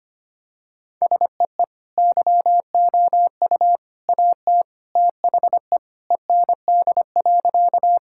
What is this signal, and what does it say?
Morse code, heard as a keyed tone, about 25 wpm, 700 Hz. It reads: SEE YOU AT THE END.
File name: practice / 25wpm / Sentences with 3-Word Phrases - 25wpm